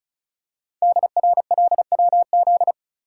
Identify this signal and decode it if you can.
Morse code, heard as a keyed tone, about 36 wpm, 690 Hz. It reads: DRLWZ
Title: Code Groups - 5 Letters - 35wpm